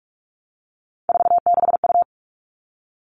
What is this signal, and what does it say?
Morse code, heard as a keyed tone, about 44 wpm, 700 Hz. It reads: 46U
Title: All Cut Numbers - 3-Digit 45wpm